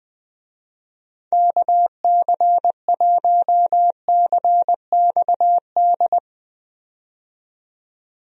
Morse code, heard as a keyed tone, about 20 wpm, 700 Hz. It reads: KC1CXD
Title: USA Call Signs - 6 Characters - 20wpm